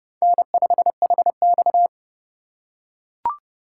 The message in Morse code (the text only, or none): N5HX